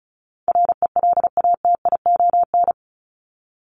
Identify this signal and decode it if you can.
Morse code, heard as a keyed tone, about 34 wpm, 700 Hz. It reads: RELATION
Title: ES:S Suffix - Top 100 Words with Base Words 35wpm